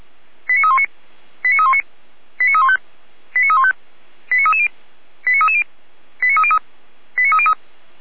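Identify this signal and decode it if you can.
Five-tone selective call (ZVEI), heard as quick runs of five tones, tones of about 70 ms. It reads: 89319 89319 89315 89315 89330 89330 89303 89303
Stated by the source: ZVEI five-tone call received by SDR.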